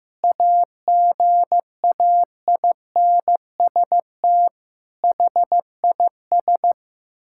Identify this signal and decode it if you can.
Morse code, heard as a keyed tone, about 15 wpm, 700 Hz. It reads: AGAINST HIS
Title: Sentences from Top 1600 Words 15wpm